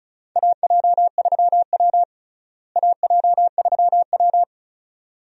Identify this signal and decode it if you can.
Morse code, heard as a keyed tone, about 36 wpm, 700 Hz. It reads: AJ3W AJ3W
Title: USA Call Signs - 4 Characters - Repeated 2x - 35wpm